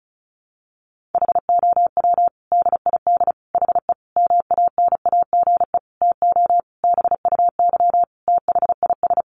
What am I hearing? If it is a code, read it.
Morse code, heard as a keyed tone, about 34 wpm, 700 Hz. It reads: HOW DID HE MANAGE TO BUY THIS
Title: Sentences from Top 1400 Words 35wpm